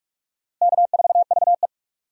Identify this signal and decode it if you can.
Morse code, heard as a keyed tone, about 42 wpm, 700 Hz. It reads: K4VE
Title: USA Call Signs - 4 Characters - Repeated 2x - 45wpm